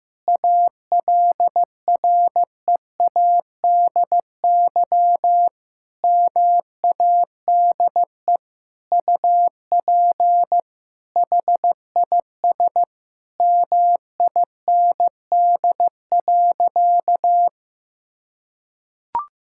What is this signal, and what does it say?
Morse code, heard as a keyed tone, about 16 wpm, 700 Hz. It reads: ALREADY MADE UP HIS MIND.
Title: Sentences from Top 1000 Words - Review Part 5 - 15wpm